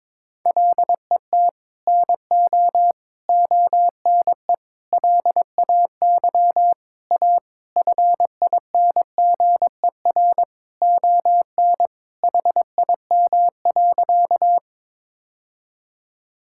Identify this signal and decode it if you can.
Morse code, heard as a keyed tone, about 22 wpm, 700 Hz. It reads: LET NO ONE LAY A FINGER ON HIM.
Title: Sentences from Top 1600 Words 22wpm